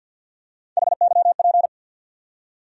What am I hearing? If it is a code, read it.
Morse code, heard as a keyed tone, about 52 wpm, 700 Hz. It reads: HYP
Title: USA Call Sign Postfixes - 3 Letters 50wpm